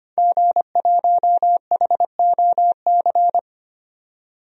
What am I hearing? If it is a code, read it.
Morse code, heard as a keyed tone, about 25 wpm, 690 Hz. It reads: G1HOC
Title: DXCC-100 Call Signs - 5 Characters - 25wpm